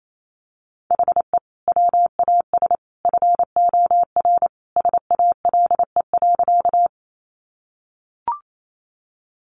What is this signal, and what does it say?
Morse code, heard as a keyed tone, about 28 wpm, 700 Hz. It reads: HE WAS FOR SALE.
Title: Sentences from Top 1400 Words 28wpm